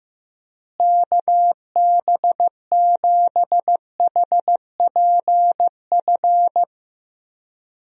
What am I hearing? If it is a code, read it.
Morse code, heard as a keyed tone, about 15 wpm, 700 Hz. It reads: KB7HPF
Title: USA Call Signs - 6 Characters - Repeated 2x - 15wpm